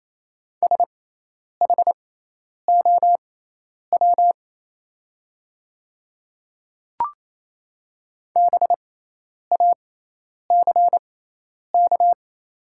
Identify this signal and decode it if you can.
Morse code, heard as a keyed tone, about 28 wpm, spaced as for 10 wpm, 690 Hz. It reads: SHOW BACK
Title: 6x Character Spacing - Top 200 Words - 28wpm